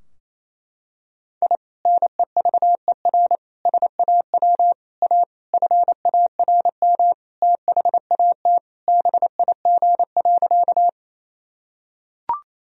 Morse code, heard as a keyed tone, about 28 wpm, 700 Hz. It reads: I NEVER SAW A FARM THAT BIG.
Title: Sentences from Top 1000 Words - Review Part 2 - 28wpm